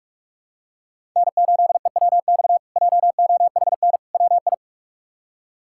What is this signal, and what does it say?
Morse code, heard as a keyed tone, about 46 wpm, 700 Hz. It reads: N8EWX JOHN WI
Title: K1USN SST QSO Practice - 45wpm